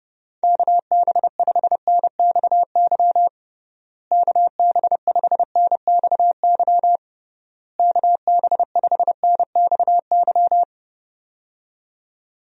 Morse code, heard as a keyed tone, about 31 wpm, 700 Hz. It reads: KB5NXY KB5NXY KB5NXY
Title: USA Call Signs - 6 Characters - Repeated 3x - 30wpm